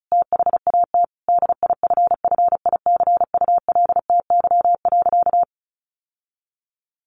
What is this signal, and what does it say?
Morse code, heard as a keyed tone, about 34 wpm, 700 Hz. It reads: THAT DIFFICULTY.
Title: Sentences from Top 1500 Words 35wpm